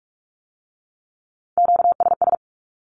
Morse code, heard as a keyed tone, about 46 wpm, 700 Hz. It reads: QSS
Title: Q-Codes with Spoken Meaning - 45wpm